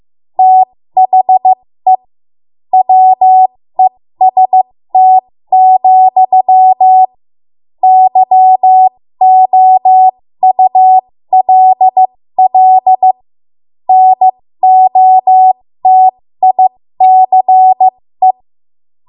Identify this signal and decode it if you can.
Morse code, heard as a keyed tone, about 15 wpm, spaced as for 13 wpm, 750 Hz. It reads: THE WEST, YOULL NOTICE
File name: Morse / 13WPM